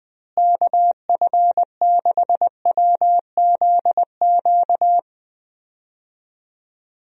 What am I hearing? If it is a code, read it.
Morse code, heard as a keyed tone, about 21 wpm, 700 Hz. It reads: KF6WZQ
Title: USA Call Signs - 6 Characters - Repeated 3x - 20wpm